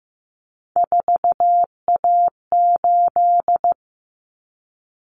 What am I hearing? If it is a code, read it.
Morse code, heard as a keyed tone, about 15 wpm, 700 Hz. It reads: 4A8
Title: Common Cut Numbers - 3-Digit 15wpm